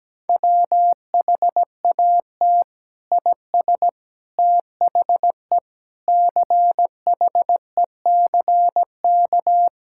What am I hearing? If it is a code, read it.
Morse code, heard as a keyed tone, about 17 wpm, 700 Hz. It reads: WHAT IS THE CHECK
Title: Sentences from Top 1000 Words - Review Part 6 - 17wpm